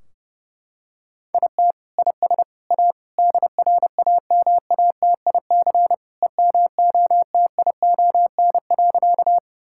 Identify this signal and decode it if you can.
Morse code, heard as a keyed tone, about 32 wpm, 700 Hz. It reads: IT IS A DRAMATIC EMOTION.
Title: Sentences from Top 1600 Words 30wpm